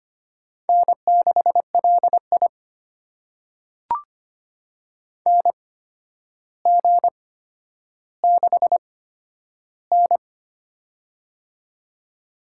Morse code, heard as a keyed tone, about 25 wpm, spaced as for 11 wpm, 700 Hz. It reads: N6LI NG6N
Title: USA Call Signs - 4 Characters - 8x Character Spacing - 25wpm